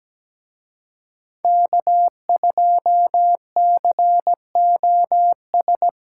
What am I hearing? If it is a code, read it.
Morse code, heard as a keyed tone, about 20 wpm, 700 Hz. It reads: K2COS
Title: Speed Racing - USA 1x3 Call Signs 17wpm